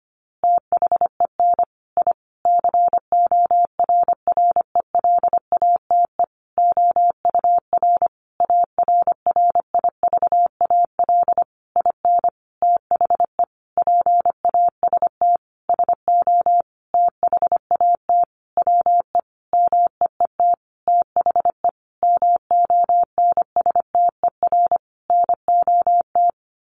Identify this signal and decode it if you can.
Morse code, heard as a keyed tone, about 25 wpm, 700 Hz. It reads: THEN I CORRELATE OUR ARRIVAL IN THE PAST SO THAT WE MEET THE MONSTER NOT
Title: A Sound Of Thunder 25wpm